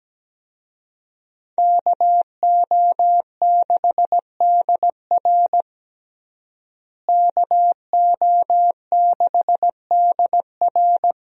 Morse code, heard as a keyed tone, about 17 wpm, 700 Hz. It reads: KO6DR KO6DR
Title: USA Call Signs - 5 Characters - Repeated 2x - 17wpm